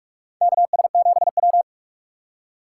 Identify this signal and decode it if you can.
Morse code, heard as a keyed tone, about 47 wpm, 700 Hz. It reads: KS7W
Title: USA Call Signs - 4 Characters - 4x Character Spacing - 45wpm